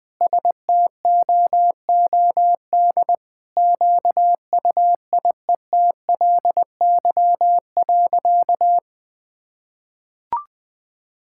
Morse code, heard as a keyed tone, about 20 wpm, 700 Hz. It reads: STOOD QUIETLY.
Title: Sentences with 2-Word Phrases - 20wpm